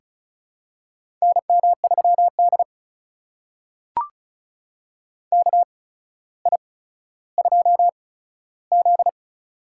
Morse code, heard as a keyed tone, about 36 wpm, spaced as for 15 wpm, 700 Hz. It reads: NM3D KI2Z
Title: USA Call Signs - 4 Characters - 8x Character Spacing - 35wpm